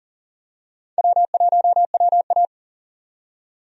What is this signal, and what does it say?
Morse code, heard as a keyed tone, about 40 wpm, 700 Hz. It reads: W1WA
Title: USA Call Signs - 4 Characters - 40wpm